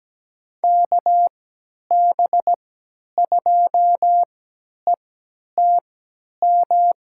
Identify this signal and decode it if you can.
Morse code, heard as a keyed tone, about 17 wpm, spaced as for 10 wpm, 700 Hz. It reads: KB2ETM
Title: USA Call Signs - 6 Characters - 3x Character Spacing - 17wpm